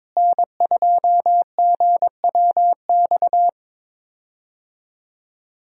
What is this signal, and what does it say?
Morse code, heard as a keyed tone, about 22 wpm, 700 Hz. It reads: N2GWX
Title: USA Call Signs - 5 Characters - Repeated 2x - 22wpm